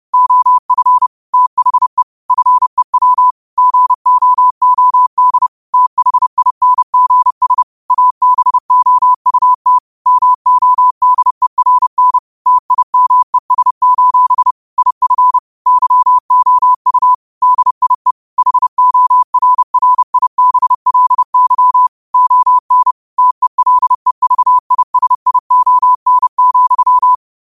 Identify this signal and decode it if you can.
Morse code, heard as a keyed tone, about 30 wpm, 1000 Hz. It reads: OF THE FEW GOOD THINGS ABOUT MODERN TIMES: IF YOU DIE HORRIBLY ON TELEVISION,
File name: qotdfiles / 30wpm